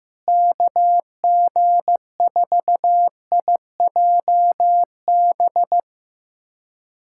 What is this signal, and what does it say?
Morse code, heard as a keyed tone, about 15 wpm, 700 Hz. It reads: KG4IJB
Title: USA Callsign DE Callsign - 6 Characters - 15wpm